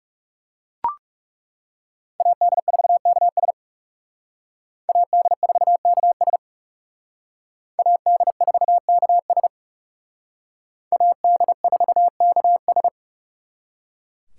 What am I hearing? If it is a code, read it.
Morse code, heard as a keyed tone, about 38 wpm, 700 Hz. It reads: AD4KS AD4KS AD4KS AD4KS